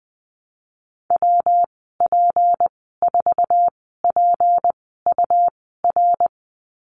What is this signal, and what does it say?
Morse code, heard as a keyed tone, about 20 wpm, spaced as for 15 wpm, 700 Hz. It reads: WP4PUR
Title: 154 Lesson 25g - Letter p - Call Signs 20wpm